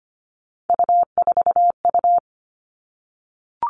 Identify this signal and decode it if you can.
Morse code, heard as a keyed tone, about 25 wpm, 700 Hz. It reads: U4U
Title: Three Challenging Letter-Number Combinations 25wpm